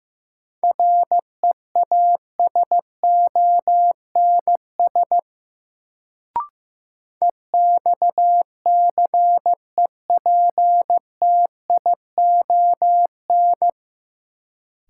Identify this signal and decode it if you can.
Morse code, heard as a keyed tone, about 16 wpm, 700 Hz. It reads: REASONS EXCEPTION
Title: Sets of 4 Words - New Word Drill - 15wpm